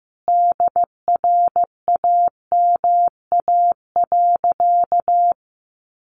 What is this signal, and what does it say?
Morse code, heard as a keyed tone, about 15 wpm, 700 Hz. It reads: DRAMA.
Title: Sentences from Top 1500 Words 15wpm